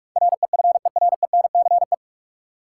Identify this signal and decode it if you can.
Morse code, heard as a keyed tone, about 44 wpm, 700 Hz. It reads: REFERENCE
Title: Sentences from Top 1200 Words - New Word Drill - 45wpm